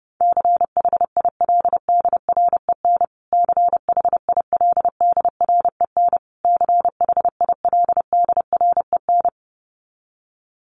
Morse code, heard as a keyed tone, about 28 wpm, 700 Hz. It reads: CHILDREN CHILDREN CHILDREN
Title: Top 300 Words Repeated 30wpm